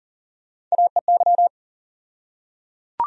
Morse code, heard as a keyed tone, about 40 wpm, 700 Hz. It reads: AEY
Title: USA Call Sign Postfixes - 3 Letters 40wpm